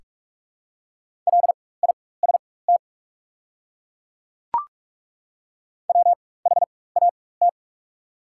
Morse code, heard as a keyed tone, about 42 wpm, spaced as for 20 wpm, 700 Hz. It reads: LIST WHAT
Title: 4x Character Spacing - Top 500 Words - 45wpm